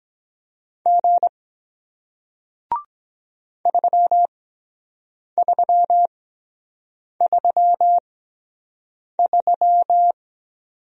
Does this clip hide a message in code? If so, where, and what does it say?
Yes: Morse code from 0.9 to 10.1 s, heard as a keyed tone, about 23 wpm, spaced as for 7 wpm, 700 Hz. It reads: G 3333